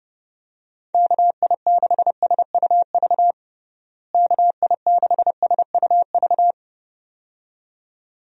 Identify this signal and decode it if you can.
Morse code, heard as a keyed tone, about 32 wpm, 700 Hz. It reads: KI6SUV KI6SUV
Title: USA Call Signs - 6 Characters - Repeated 2x - 30wpm